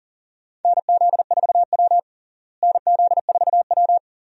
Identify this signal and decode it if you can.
Morse code, heard as a keyed tone, about 40 wpm, 700 Hz. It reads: NZ4W NZ4W
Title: USA Call Signs - 4 Characters - Repeated 3x - 40wpm